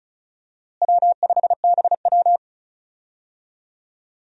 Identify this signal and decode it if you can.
Morse code, heard as a keyed tone, about 36 wpm, 700 Hz. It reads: W5BW
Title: USA Call Signs - 4 Characters - Repeated 2x - 35wpm